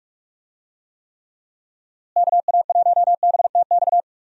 Morse code, heard as a keyed tone, about 42 wpm, 700 Hz. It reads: KA1BTX
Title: USA Call Signs - 6 Characters - Repeated 2x - 45wpm